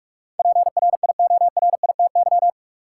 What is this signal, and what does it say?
Morse code, heard as a keyed tone, about 46 wpm, 700 Hz. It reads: PRIORITY
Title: Sentences from Top 1600 Words - New Word Drill - 45wpm